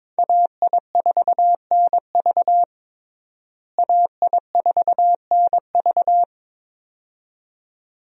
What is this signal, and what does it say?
Morse code, heard as a keyed tone, about 22 wpm, 700 Hz. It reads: AI4NV AI4NV
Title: USA Call Signs - 5 Characters - Repeated 2x - 22wpm